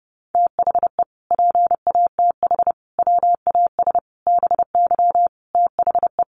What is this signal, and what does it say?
Morse code, heard as a keyed tone, about 30 wpm, 700 Hz. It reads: THE PATH WAS BY THE
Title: Sentences from Top 1400 Words 30wpm